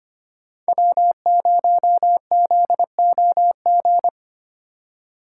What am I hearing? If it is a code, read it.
Morse code, heard as a keyed tone, about 25 wpm, 700 Hz. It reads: W0ZOG
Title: USA Call Signs - 5 Characters - 2x Character Spacing - 25wpm